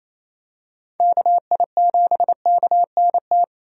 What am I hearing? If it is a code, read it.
Morse code, heard as a keyed tone, about 27 wpm, 700 Hz. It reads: KI7KNT
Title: USA Call Signs - 6 Characters - 28wpm